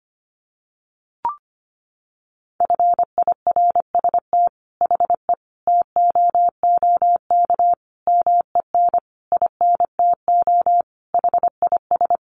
Morse code, heard as a keyed tone, about 25 wpm, 700 Hz. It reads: FIRST HE TOOK MEN INTO HIS
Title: Sentences from Top 1500 Words 25wpm